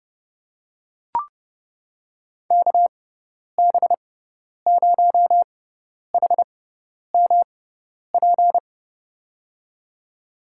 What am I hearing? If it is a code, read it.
Morse code, heard as a keyed tone, about 31 wpm, spaced as for 10 wpm, 700 Hz. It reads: KB0HMP
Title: USA Call Signs - 6x Character Spacing - 30wpm